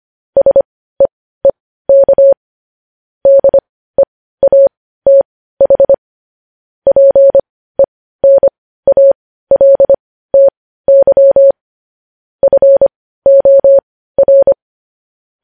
Morse code, heard as a keyed tone, about 25 wpm, spaced as for 15 wpm, 550 Hz. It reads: SEEK DEATH PENALTY FOR
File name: podcast / News headlines-1x-25x15